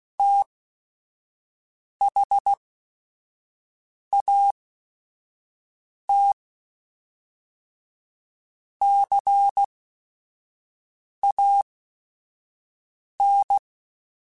Morse code, heard as a keyed tone, about 16 wpm, spaced as for 5 wpm, 800 Hz. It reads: THAT CAN
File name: Technical Subjects / Technical Subjects-mp3s / OR Gate @5